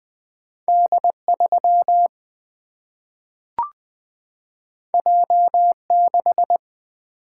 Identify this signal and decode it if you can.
Morse code, heard as a keyed tone, about 20 wpm, 700 Hz. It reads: D3 J6
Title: Single-Letter Single-Number 20wpm